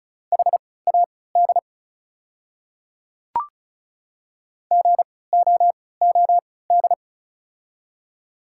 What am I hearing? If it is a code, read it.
Morse code, heard as a keyed tone, about 36 wpm, spaced as for 20 wpm, 700 Hz. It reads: HAD GOOD
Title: 3x Character Spacing - Top 200 Words - 35wpm